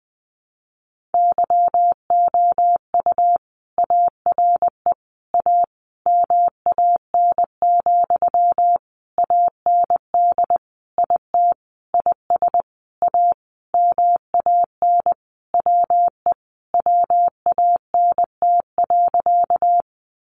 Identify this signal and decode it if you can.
Morse code, heard as a keyed tone, about 20 wpm, 700 Hz. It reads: YOU ARE A MAN, AND IT IS A MAN WE WANT.
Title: Sentences from Top 200 Words 20wpm